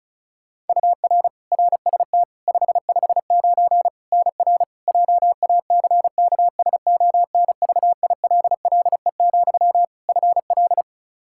Morse code, heard as a keyed tone, about 34 wpm, 700 Hz. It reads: UR RST 559 NR JACKSONVILLE, FL
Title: Practice QSO - Exchange 1 - Beginning - 35wpm